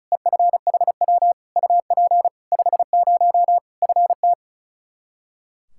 Morse code, heard as a keyed tone, about 36 wpm, 700 Hz. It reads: EFHW UP 50 FT
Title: Practice QSO - Exchange 2 - Info - 35wpm